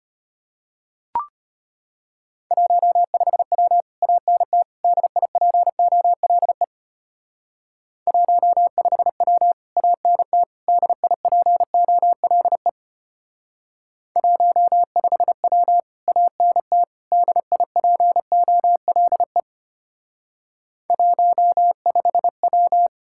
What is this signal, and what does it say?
Morse code, heard as a keyed tone, about 31 wpm, 700 Hz. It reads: 15W ANT DIPOLE 15W ANT DIPOLE 15W ANT DIPOLE 15W
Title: Speed-Racing Common QSO Phrases - 25wpm